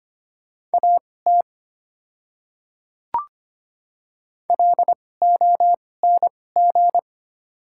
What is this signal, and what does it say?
Morse code, heard as a keyed tone, about 25 wpm, spaced as for 18 wpm, 690 Hz. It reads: AT LONG
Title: 2x Character Spacing - Top 100 Words - 25wpm